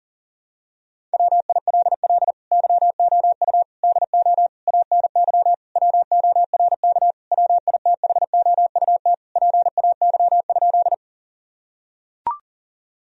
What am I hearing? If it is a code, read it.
Morse code, heard as a keyed tone, about 37 wpm, 700 Hz. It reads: WILL YOU DO ANY WORK WITHOUT PAY?
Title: Sentences from Top 1000 Words - Review Part 3 - 40wpm